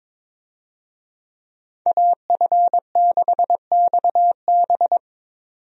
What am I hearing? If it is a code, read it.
Morse code, heard as a keyed tone, about 21 wpm, 700 Hz. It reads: AF6XB
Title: USA Call Signs - 5 Characters - 22wpm